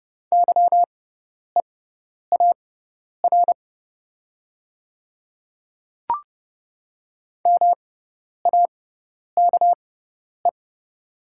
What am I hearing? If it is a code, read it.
Morse code, heard as a keyed tone, about 31 wpm, spaced as for 10 wpm, 700 Hz. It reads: YEAR MAKE
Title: 6x Character Spacing - Top 200 Words - 30wpm